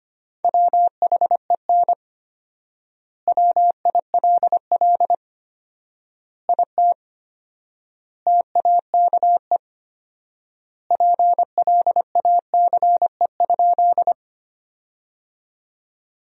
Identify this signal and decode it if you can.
Morse code, heard as a keyed tone, about 25 wpm, 700 Hz. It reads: WHEN WILL IT TAKE PLACE?
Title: 4x Word-Spacing - Sentences from Top 200 Words - 25wpm